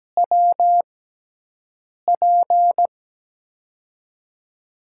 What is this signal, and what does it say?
Morse code, heard as a keyed tone, about 17 wpm, 700 Hz. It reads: W P
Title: USA Call Signs - 6x Character Spacing - 17wpm